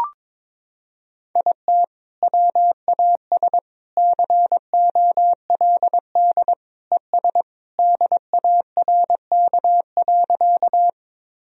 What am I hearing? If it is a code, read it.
Morse code, heard as a keyed tone, about 22 wpm, 700 Hz. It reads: IT WAS COLD ES DARK.
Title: Sentences from Top 500 Words and Abbreviations 22wpm